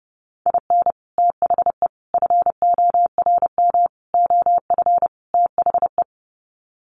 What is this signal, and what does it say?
Morse code, heard as a keyed tone, about 30 wpm, 700 Hz. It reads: IN THE FORM OF THE
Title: Sets of 5 Words Encore2 30wpm